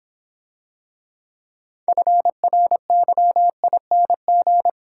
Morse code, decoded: FRYING

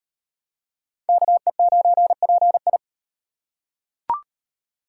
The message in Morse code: KE9PI

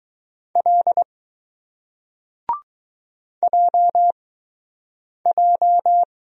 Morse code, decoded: L JJ

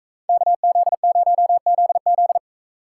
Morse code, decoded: KZ0ZZ

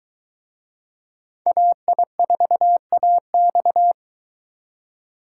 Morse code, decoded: AI4AX

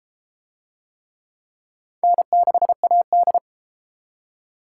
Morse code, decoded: N6AD

